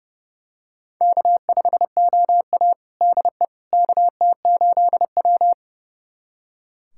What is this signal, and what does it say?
Morse code, heard as a keyed tone, about 30 wpm, 700 Hz. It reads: K5OA DE KT8W
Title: USA Callsign DE Callsign - 4 Characters - 30wpm